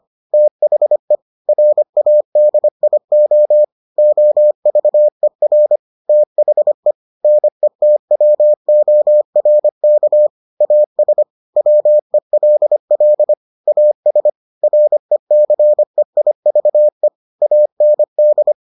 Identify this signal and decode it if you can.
Morse code, heard as a keyed tone, about 25 wpm, 600 Hz. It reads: THE RADIO OVER THE NETWORK AS WELL AS RECEIVE AND